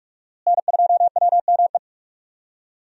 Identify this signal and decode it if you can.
Morse code, heard as a keyed tone, about 44 wpm, 700 Hz. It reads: N2WME
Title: USA Call Signs - 5 Characters - Repeated 2x - 45wpm